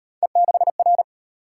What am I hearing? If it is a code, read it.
Morse code, heard as a keyed tone, about 30 wpm, 700 Hz. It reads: E6R
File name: Speed-Racing - Three Letter-Number Combinations - 25wpm